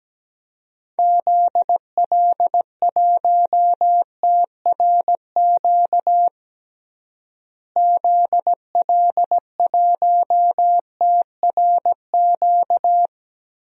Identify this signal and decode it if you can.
Morse code, heard as a keyed tone, about 17 wpm, 700 Hz. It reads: ZL1TRQ ZL1TRQ
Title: DXCC-100 Call Signs - 6 Characters - Repeated 2x - 17wpm